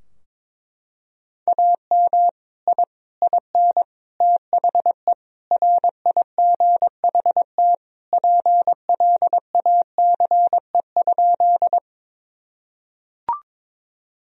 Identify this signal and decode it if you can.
Morse code, heard as a keyed tone, about 22 wpm, 700 Hz. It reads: AM I IN THE RIGHT PLACE?
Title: Sentences from Top 1000 Words - Review Part 2 - 22wpm